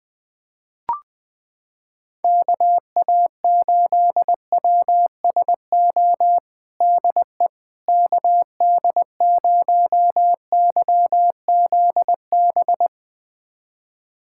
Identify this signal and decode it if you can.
Morse code, heard as a keyed tone, about 20 wpm, 700 Hz. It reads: KA8WSO DE KD0YZB
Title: USA Callsign DE Callsign - 6 Characters - 20wpm